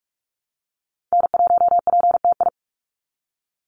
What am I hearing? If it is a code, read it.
Morse code, heard as a keyed tone, about 44 wpm, 700 Hz. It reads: N1PTI
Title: USA Call Signs 45wpm